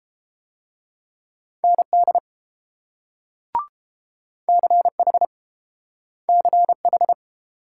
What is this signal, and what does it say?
Morse code, heard as a keyed tone, about 30 wpm, 690 Hz. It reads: ND CH CH